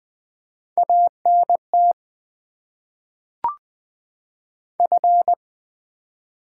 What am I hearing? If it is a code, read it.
Morse code, heard as a keyed tone, about 20 wpm, 690 Hz. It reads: ANT F
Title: Top 20 QSO Elements - 6x Character Spacing - 20wpm